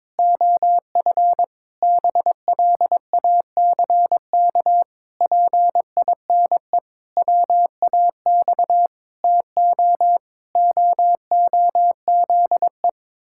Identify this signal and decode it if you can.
Morse code, heard as a keyed tone, about 22 wpm, 700 Hz. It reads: OF BLACK PINE WAX TO OOZE
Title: Pangram Challenge v2 - 22wpm